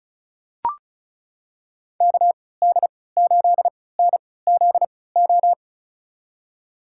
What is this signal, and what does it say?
Morse code, heard as a keyed tone, about 36 wpm, spaced as for 20 wpm, 700 Hz. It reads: KD8NZO
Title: USA Call Signs - 6 Characters - 3x Character Spacing - 35wpm